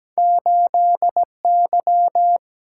Morse code, heard as a keyed tone, about 17 wpm, 700 Hz. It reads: 8 Y
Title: Single-Letter Single-Number 17wpm